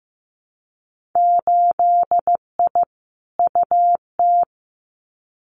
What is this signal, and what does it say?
Morse code, heard as a keyed tone, about 15 wpm, 700 Hz. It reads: 8I UT
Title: Winter Field Day Exchanges Encore 15wpm